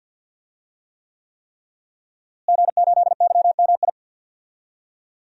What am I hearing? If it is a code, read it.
Morse code, heard as a keyed tone, about 46 wpm, 700 Hz. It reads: G8YMI
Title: DXCC-100 Call Signs - 5 Characters - Repeated 3x - 50wpm